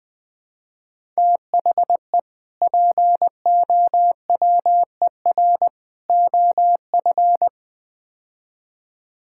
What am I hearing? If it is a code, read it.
Morse code, heard as a keyed tone, about 20 wpm, 700 Hz. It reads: THE POWER OF
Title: Build A Phrase - 4-Word Phrases - 20wpm